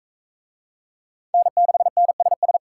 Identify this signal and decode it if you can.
Morse code, heard as a keyed tone, about 39 wpm, 700 Hz. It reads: N6NSS